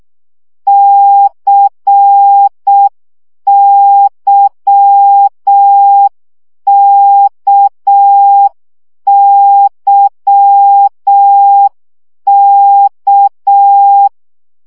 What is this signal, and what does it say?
Morse code, heard as a keyed tone, about 6 wpm, 800 Hz. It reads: CYKYK
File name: CW gyakorlas k c y 5 baud